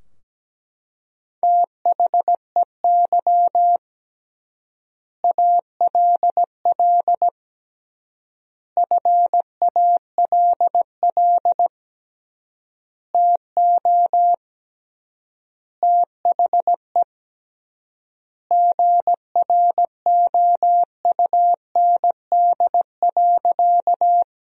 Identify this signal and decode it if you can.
Morse code, heard as a keyed tone, about 17 wpm, 700 Hz. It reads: THEY ALL FALL TO THE GROUND.
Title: 3x Word-Spacing - Sentences from Top 500 Words - 17wpm